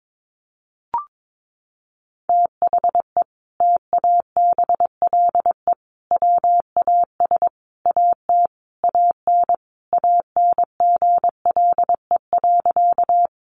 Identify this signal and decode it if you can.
Morse code, heard as a keyed tone, about 22 wpm, 700 Hz. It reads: THE TABLE WAS AT AN ANGLE.
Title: Sentences from Top 1500 Words 22wpm